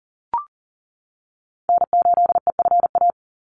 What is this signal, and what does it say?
Morse code, heard as a keyed tone, about 40 wpm, 700 Hz. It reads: N8EFA
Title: USA Call Signs 40wpm